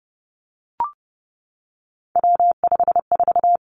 Morse code, heard as a keyed tone, about 30 wpm, 700 Hz. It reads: W54
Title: Three Challenging Letter-Number Combinations 30wpm